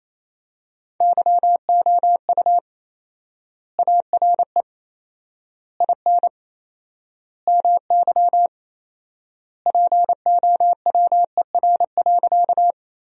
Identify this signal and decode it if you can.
Morse code, heard as a keyed tone, about 27 wpm, 700 Hz. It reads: YOU ARE IN MY POWER.
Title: 4x Word-Spacing - Sentences from Top 500 Words - 28wpm